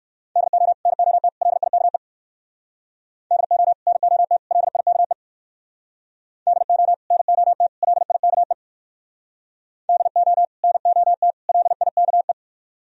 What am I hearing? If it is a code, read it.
Morse code, heard as a keyed tone, about 59 wpm, 700 Hz. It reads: DO NOT LIKE DO NOT LIKE DO NOT LIKE DO NOT LIKE